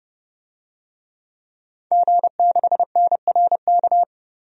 Morse code, heard as a keyed tone, about 30 wpm, 700 Hz. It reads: G6NRK
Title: DXCC-100 Call Signs - 5 Characters - 30wpm